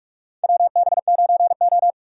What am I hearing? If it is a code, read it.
Morse code, heard as a keyed tone, about 44 wpm, 700 Hz. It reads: WB9O
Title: USA Call Signs - 4 Characters - 3x Character Spacing - 45wpm